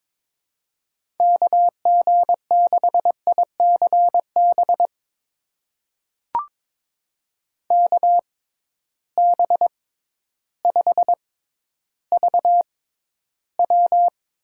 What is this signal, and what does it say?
Morse code, heard as a keyed tone, about 22 wpm, spaced as for 12 wpm, 700 Hz. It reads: KG6ICB KB5VW
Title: USA Call Signs - 6x Character Spacing - 22wpm